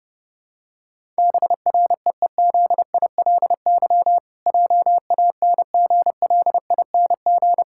Morse code, heard as a keyed tone, about 32 wpm, 700 Hz. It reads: BREEZILY JANGLING
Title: Pangram Challenge v2 - 30wpm